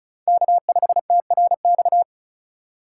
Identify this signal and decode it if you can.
Morse code, heard as a keyed tone, about 36 wpm, 700 Hz. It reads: K5TRX
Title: USA Call Signs - 5 Characters - 2x Character Spacing - 35wpm